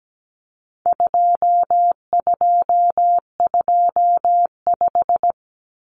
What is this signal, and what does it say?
Morse code, heard as a keyed tone, about 17 wpm, 700 Hz. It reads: 2225
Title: Common Cut Numbers - 4-Digit 17wpm